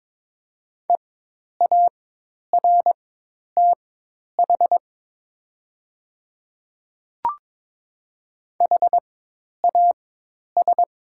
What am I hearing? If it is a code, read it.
Morse code, heard as a keyed tone, about 22 wpm, spaced as for 10 wpm, 700 Hz. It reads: EARTH HAS
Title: 4x Character Spacing - Top 300 Words - 22wpm